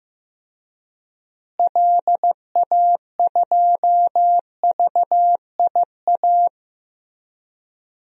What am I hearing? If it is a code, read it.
Morse code, heard as a keyed tone, about 15 wpm, 700 Hz. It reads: LA2VIA
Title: DXCC-100 Call Signs - 6 Characters - Repeated 3x - 15wpm